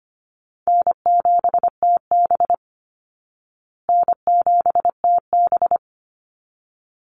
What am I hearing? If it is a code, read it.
Morse code, heard as a keyed tone, about 25 wpm, 700 Hz. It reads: N7TB N7TB
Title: Easy Call Signs Repeated 2x - 25wpm